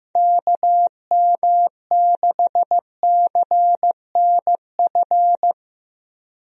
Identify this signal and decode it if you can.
Morse code, heard as a keyed tone, about 15 wpm, 700 Hz. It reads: KM6CNF
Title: USA Call Signs - 6 Characters - Repeated 2x - 15wpm